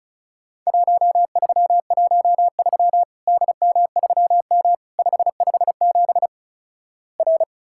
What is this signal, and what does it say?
Morse code, heard as a keyed tone, about 36 wpm, 700 Hz. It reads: 1313 DM3M 557 R
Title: Worked All Europe DX Contest - QTC Exchange - 35wpm